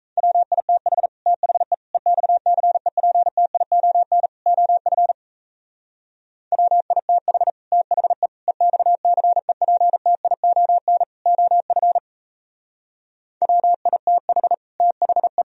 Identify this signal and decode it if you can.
Morse code, heard as a keyed tone, about 35 wpm, 700 Hz. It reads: WITH THE EXCEPTION OF WITH THE EXCEPTION OF WITH THE
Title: Speed-Racing - Sets of 4 Words 28wpm